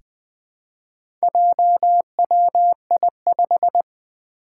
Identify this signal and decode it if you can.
Morse code, heard as a keyed tone, about 20 wpm, 700 Hz. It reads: JWI5
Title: Four Challenging Letter-Number Combinations 20wpm